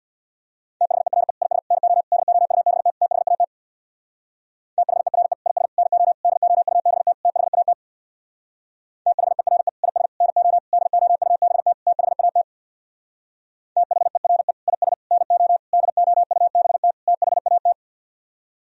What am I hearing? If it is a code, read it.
Morse code, heard as a keyed tone, about 64 wpm, 700 Hz. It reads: THERE IS NO DOUBT THAT THERE IS NO DOUBT THAT THERE IS NO DOUBT THAT THERE IS NO DOUBT THAT